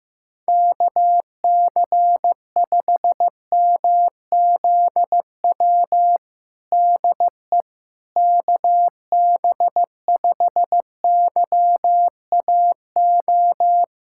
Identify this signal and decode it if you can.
Morse code, heard as a keyed tone, about 15 wpm, 700 Hz. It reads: KC5MZW DE KB5YAO
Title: USA Callsign DE Callsign - 6 Characters - 15wpm